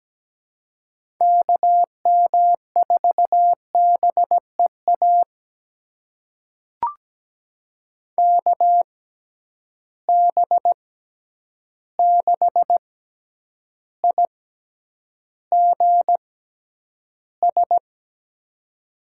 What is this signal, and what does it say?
Morse code, heard as a keyed tone, about 17 wpm, spaced as for 9 wpm, 700 Hz. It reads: KM4BEA KB6IGS
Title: USA Call Signs - 6 Characters - 6x Character Spacing - 17wpm